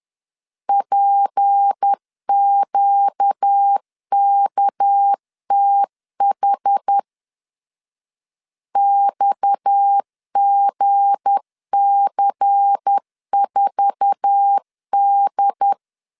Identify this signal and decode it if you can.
Morse code, heard as a keyed tone, about 10 wpm, 800 Hz. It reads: PQKTH XGC4D
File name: LEZIONI / 8 YJ45 / 8 YJ4 10wpm 2633